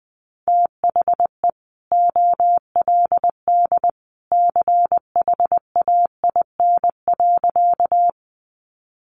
Morse code, heard as a keyed tone, about 20 wpm, 700 Hz. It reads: THE OLD CHAIN.
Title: Sentences from Top 1500 Words 20wpm